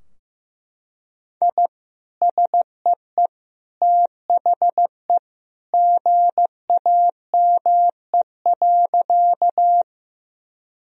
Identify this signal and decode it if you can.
Morse code, heard as a keyed tone, about 15 wpm, 700 Hz. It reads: I SEE THE GAME.
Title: Sentences from Top 1000 Words - Review Part 3 - 15wpm